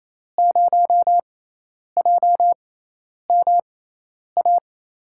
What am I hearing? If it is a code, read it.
Morse code, heard as a keyed tone, about 28 wpm, spaced as for 10 wpm, 700 Hz. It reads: 0JMA
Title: USA Call Signs - 6 Characters - 6x Character Spacing - 28wpm